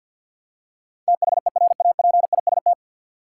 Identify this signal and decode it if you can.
Morse code, heard as a keyed tone, about 53 wpm, 700 Hz. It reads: THERAPIST
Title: Practice QSO - Exchange 3 - More Info - 50wpm